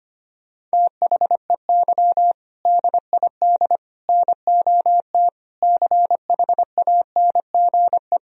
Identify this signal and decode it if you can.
Morse code, heard as a keyed tone, about 25 wpm, 700 Hz. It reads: THEY DID NOT CHANGE
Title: Sentences from Top 1000 Words - Review Part 1 - 25wpm